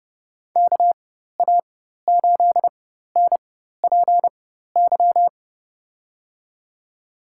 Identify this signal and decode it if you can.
Morse code, heard as a keyed tone, about 30 wpm, spaced as for 14 wpm, 700 Hz. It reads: KA8NPY
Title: USA Call Signs - 4x Character Spacing - 30wpm